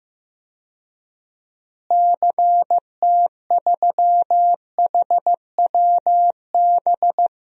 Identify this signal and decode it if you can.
Morse code, heard as a keyed tone, about 15 wpm, 700 Hz. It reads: CT3HWB
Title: DXCC-100 Call Signs - 6 Characters - Repeated 3x - 15wpm